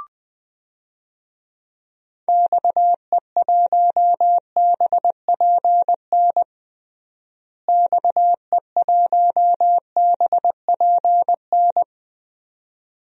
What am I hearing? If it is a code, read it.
Morse code, heard as a keyed tone, about 20 wpm, 700 Hz. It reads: XE1BPN XE1BPN
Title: DXCC-100 Call Signs - 6 Characters - Repeated 2x - 20wpm